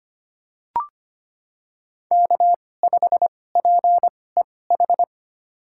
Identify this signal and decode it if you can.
Morse code, heard as a keyed tone, about 25 wpm, spaced as for 18 wpm, 700 Hz. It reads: K5PEH
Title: USA Call Signs - 5 Characters - 2x Character Spacing - 25wpm